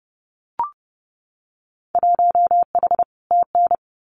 Morse code, heard as a keyed tone, about 30 wpm, 700 Hz. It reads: 1H TN
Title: Winter Field Day Exchanges 30wpm